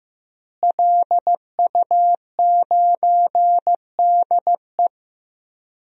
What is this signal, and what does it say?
Morse code, heard as a keyed tone, about 16 wpm, 700 Hz. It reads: LU9DE
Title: DXCC-100 Call Signs - 5 Characters - Repeated 3x - 15wpm